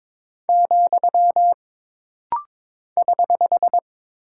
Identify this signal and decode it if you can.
Morse code, heard as a keyed tone, about 22 wpm, 700 Hz. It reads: , <HH>
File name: practice / 22wpm / Single Punctuation-Prosign - Rapid-Fire - 22wpm